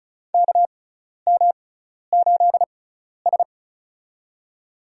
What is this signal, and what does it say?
Morse code, heard as a keyed tone, about 34 wpm, spaced as for 12 wpm, 700 Hz. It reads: KM8S
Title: USA Call Signs - 4 Characters - 6x Character Spacing - 35wpm